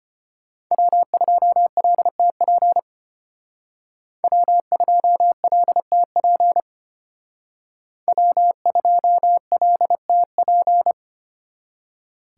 Morse code, heard as a keyed tone, about 32 wpm, 700 Hz. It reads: W2LTP W2LTP W2LTP